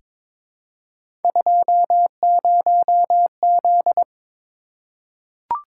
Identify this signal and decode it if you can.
Morse code, heard as a keyed tone, about 22 wpm, 700 Hz. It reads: 20Z
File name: practice / 22wpm / Three Letter-Number Combinations - 22wpm